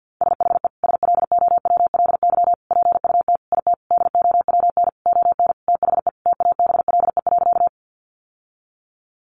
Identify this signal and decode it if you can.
Morse code, heard as a keyed tone, about 50 wpm, 700 Hz. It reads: SHE SLOWLY PUT IT DOWN ON THE TABLE.
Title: Sentences from Top 1100 Words 50wpm